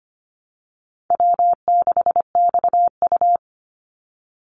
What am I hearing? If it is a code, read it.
Morse code, heard as a keyed tone, about 25 wpm, 700 Hz. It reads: W6XU
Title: Easy Call Signs - 25wpm